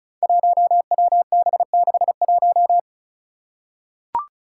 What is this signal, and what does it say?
Morse code, heard as a keyed tone, about 36 wpm, 700 Hz. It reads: 1WB61
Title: Five Challenging Letter-Number Combinations 35wpm